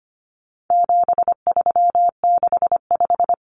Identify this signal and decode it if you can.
Morse code, heard as a keyed tone, about 25 wpm, 700 Hz. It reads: 7365
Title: Common Cut Numbers - 4-Digit 25wpm